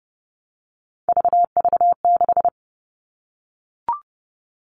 Morse code, heard as a keyed tone, about 30 wpm, 700 Hz. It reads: VV6
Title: All Cut Numbers - 3-Digit 30wpm